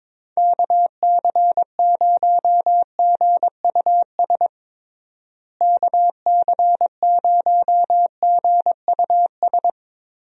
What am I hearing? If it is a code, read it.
Morse code, heard as a keyed tone, about 22 wpm, 700 Hz. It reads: KC0GUS KC0GUS
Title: USA Call Signs - 6 Characters - Repeated 2x - 22wpm